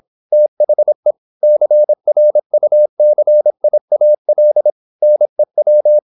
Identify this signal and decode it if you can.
Morse code, heard as a keyed tone, about 26 wpm, 600 Hz. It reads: THE CRUCIAL NEW
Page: Never 26wpm | CW med Gnister